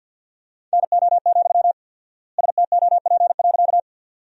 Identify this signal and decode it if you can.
Morse code, heard as a keyed tone, about 50 wpm, 700 Hz. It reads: NO, STOP.
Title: 4x Word-Spacing - Sentences from Top 300 Words - 50wpm